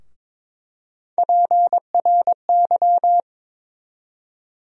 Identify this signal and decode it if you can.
Morse code, heard as a keyed tone, about 21 wpm, 700 Hz. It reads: PRY
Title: USA Call Sign Postfixes - 3 Letters 22wpm